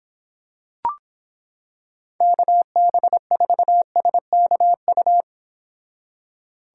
Morse code, heard as a keyed tone, about 26 wpm, 700 Hz. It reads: KB4SKU